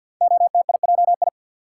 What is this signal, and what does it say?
Morse code, heard as a keyed tone, about 46 wpm, 700 Hz. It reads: OTIJI
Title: Code Groups - 5 Letters - 50wpm